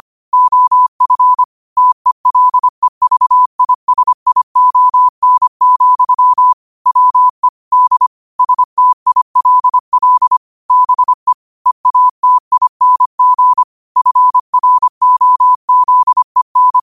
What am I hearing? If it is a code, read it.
Morse code, heard as a keyed tone, about 25 wpm, 1000 Hz. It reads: OF TELEVISION, WE D STILL BE EATING FROZEN
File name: qotdfiles / 25wpm